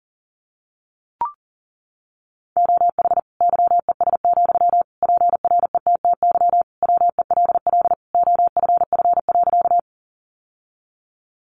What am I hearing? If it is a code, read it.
Morse code, heard as a keyed tone, about 40 wpm, 700 Hz. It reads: OH YES, PRETTY WELL OFF.
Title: Sentences from Top 1000 Words 40wpm